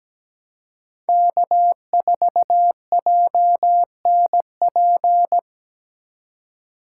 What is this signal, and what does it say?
Morse code, heard as a keyed tone, about 17 wpm, 700 Hz. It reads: K4JNP
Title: USA Call Signs - 3x Character Spacing - 17wpm